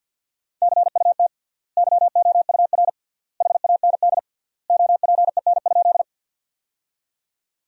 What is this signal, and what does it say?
Morse code, heard as a keyed tone, about 50 wpm, 700 Hz. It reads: CUT YOUR HAND OPEN?
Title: 3x Word-Spacing - Sentences from Top 300 Words - 50wpm